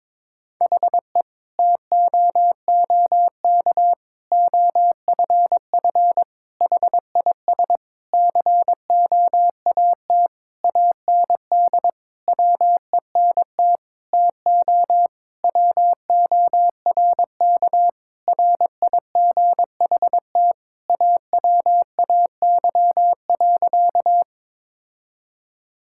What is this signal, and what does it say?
Morse code, heard as a keyed tone, about 22 wpm, 700 Hz. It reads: HE TOOK OFF HIS COAT AND WENT TO WORK RIGHT AWAY.
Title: Sentences from Top 1600 Words 22wpm